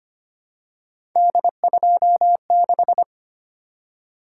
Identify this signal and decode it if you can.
Morse code, heard as a keyed tone, about 25 wpm, 700 Hz. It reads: D26
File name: Three Letter-Number Combinations - 25wpm